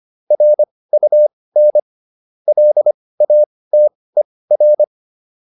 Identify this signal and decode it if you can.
Morse code, heard as a keyed tone, about 25 wpm, spaced as for 18 wpm, 600 Hz. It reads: RUN LATER